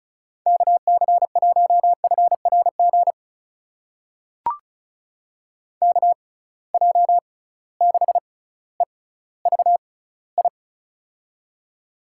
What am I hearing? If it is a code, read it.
Morse code, heard as a keyed tone, about 34 wpm, spaced as for 18 wpm, 700 Hz. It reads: KC1FRG KJ6EVI
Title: USA Call Signs - 6 Characters - 6x Character Spacing - 35wpm